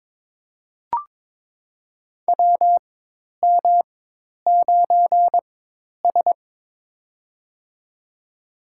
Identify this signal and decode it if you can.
Morse code, heard as a keyed tone, about 22 wpm, spaced as for 10 wpm, 700 Hz. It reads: WM9S
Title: USA Call Signs - 4x Character Spacing - 22wpm